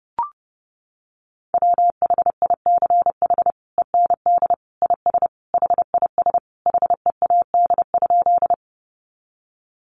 Morse code, heard as a keyed tone, about 30 wpm, 700 Hz. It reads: WHICH END IS HIS HEAD?
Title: Sentences from Top 300 Words 30wpm